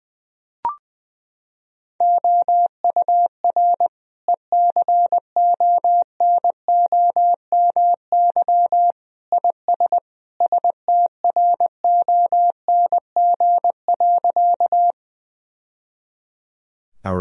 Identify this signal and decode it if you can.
Morse code, heard as a keyed tone, about 19 wpm, 700 Hz. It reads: OUR ECONOMY IS STRONG.
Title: Sentences from Top 1000 Words - Review Part 3 - 20wpm